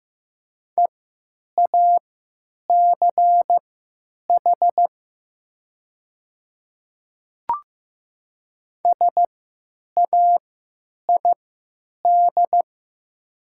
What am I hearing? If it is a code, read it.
Morse code, heard as a keyed tone, about 15 wpm, spaced as for 9 wpm, 690 Hz. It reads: EACH SAID
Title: 3x Character Spacing - Top 100 Words - 15wpm